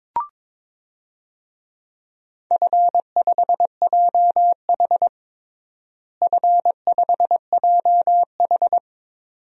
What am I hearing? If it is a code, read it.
Morse code, heard as a keyed tone, about 22 wpm, 700 Hz. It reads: F5JH F5JH
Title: DXCC-100 Call Signs - 4 Characters - Repeated 2x - 22wpm